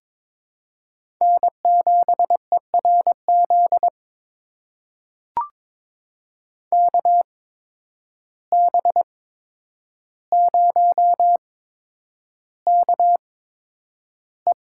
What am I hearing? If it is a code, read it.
Morse code, heard as a keyed tone, about 21 wpm, spaced as for 9 wpm, 700 Hz. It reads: N7ERZ KB0KE
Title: USA Call Signs - 5 Characters - 8x Character Spacing - 22wpm